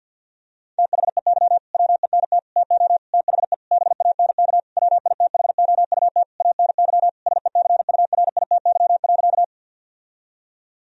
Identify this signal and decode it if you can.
Morse code, heard as a keyed tone, about 50 wpm, 700 Hz. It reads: THEY WENT TO THE BANK WITHOUT ANY SECURITY.
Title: Sentences from Top 1000 Words - Review Part 4 - 50wpm